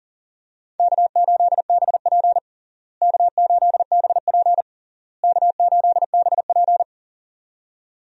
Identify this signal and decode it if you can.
Morse code, heard as a keyed tone, about 40 wpm, 700 Hz. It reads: K8BP K8BP K8BP
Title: USA Call Signs - 4 Characters - Repeated 3x - 40wpm